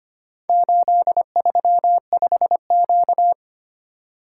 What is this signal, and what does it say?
Morse code, heard as a keyed tone, about 25 wpm, 700 Hz. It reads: 835Q
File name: practice / 25wpm / Four Letter-Number Combinations - 25wpm